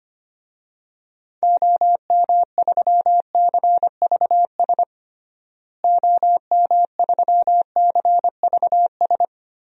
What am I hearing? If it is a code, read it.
Morse code, heard as a keyed tone, about 25 wpm, 700 Hz. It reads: OM3CVS OM3CVS
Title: DXCC-100 Call Signs - 6 Characters - Repeated 3x - 25wpm